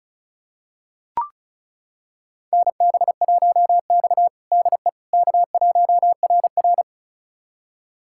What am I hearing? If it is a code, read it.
Morse code, heard as a keyed tone, about 36 wpm, 700 Hz. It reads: NB1X DE K1RR
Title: USA Callsign DE Callsign - 4 Characters - 35wpm